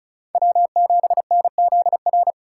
Morse code, heard as a keyed tone, about 36 wpm, 700 Hz. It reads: W7NZR
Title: USA Call Signs - 5 Characters - 3x Character Spacing - 35wpm